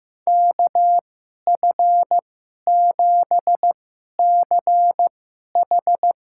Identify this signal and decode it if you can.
Morse code, heard as a keyed tone, about 16 wpm, spaced as for 11 wpm, 700 Hz. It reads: KF7CH
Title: USA Call Signs - 2x Character Spacing - 15wpm